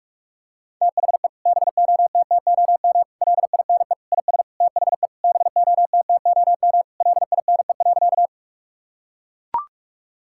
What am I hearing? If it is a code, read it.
Morse code, heard as a keyed tone, about 44 wpm, 700 Hz. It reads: THE BOTTOM LINE IS THE BOTTOM LINE.
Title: Sentences from Top 1000 Words - Review Part 6 - 45wpm